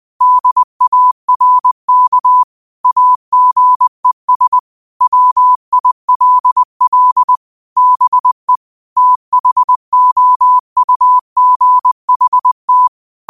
Morse code, heard as a keyed tone, about 20 wpm, 1000 Hz. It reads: DARK AGES WILL BE THOUGHT
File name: 2025-06-25-20wpm